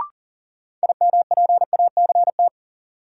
Morse code, heard as a keyed tone, about 40 wpm, 700 Hz. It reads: IMPACT
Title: Sentences from Top 1000 Words - Review Part 6 - New Word Drill - 40wpm